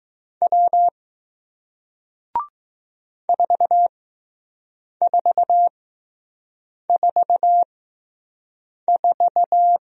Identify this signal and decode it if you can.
Morse code, heard as a keyed tone, about 20 wpm, spaced as for 6 wpm, 700 Hz. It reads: W 4444